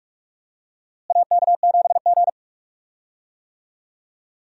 Morse code, heard as a keyed tone, about 42 wpm, 700 Hz. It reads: AK7G
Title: USA Call Signs - 4 Characters - Repeated 2x - 45wpm